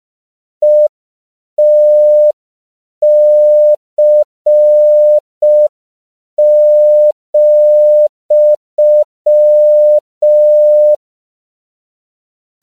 Morse code, heard as a keyed tone, about 5 wpm, 600 Hz. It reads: ETC,
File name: CW / 5PPM